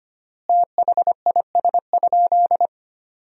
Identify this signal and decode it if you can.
Morse code, heard as a keyed tone, about 25 wpm, 700 Hz. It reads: THIS?
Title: 3x Word-Spacing - Sentences from Top 100 Words - 25wpm